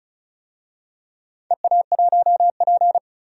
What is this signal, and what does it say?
Morse code, heard as a keyed tone, about 34 wpm, 700 Hz. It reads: EA1P
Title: DXCC-100 Call Signs - 4 Characters - 35wpm